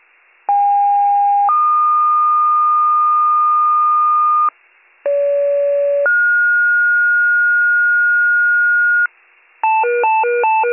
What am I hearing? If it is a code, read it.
Two-tone page, heard as a one-second tone, then a three-second tone, about 810 then 1220 Hz, about 570 then 1430 Hz, with noise. It is heard from 0.5 to 9.1 s.